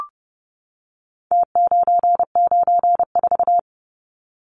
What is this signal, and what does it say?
Morse code, heard as a keyed tone, about 30 wpm, 700 Hz. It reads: T994
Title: Numbers - Contest Style - 4-Digit 30wpm